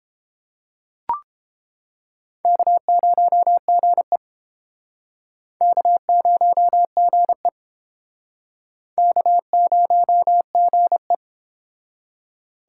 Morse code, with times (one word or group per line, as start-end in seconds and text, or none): 2.5-4.2 K0GE
5.6-7.5 K0GE
9.0-11.1 K0GE